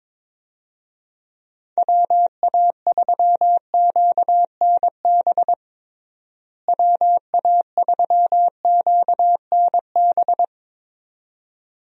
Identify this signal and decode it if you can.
Morse code, heard as a keyed tone, about 22 wpm, 700 Hz. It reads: WA3QNB WA3QNB
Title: USA Call Signs - 6 Characters - Repeated 2x - 22wpm